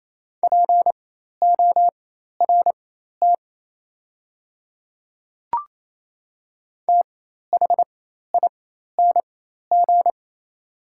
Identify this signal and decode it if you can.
Morse code, heard as a keyed tone, about 28 wpm, spaced as for 13 wpm, 700 Hz. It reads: PORT THING
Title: 4x Character Spacing - Top 300 Words - 28wpm